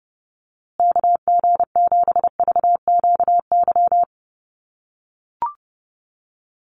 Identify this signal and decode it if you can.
Morse code, heard as a keyed tone, about 30 wpm, 700 Hz. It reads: KG7VQY
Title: USA Call Signs Repeated 3x 30wpm